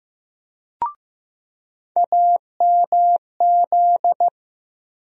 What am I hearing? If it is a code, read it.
Morse code, heard as a keyed tone, about 15 wpm, 700 Hz. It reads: AMZ
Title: USA Call Sign Postfixes - 3 Letters 15wpm